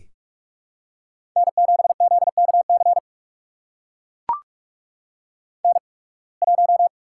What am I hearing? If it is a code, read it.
Morse code, heard as a keyed tone, about 44 wpm, spaced as for 29 wpm, 690 Hz. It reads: N7ZKC N1
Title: USA Call Signs - 5 Characters - 8x Character Spacing - 45wpm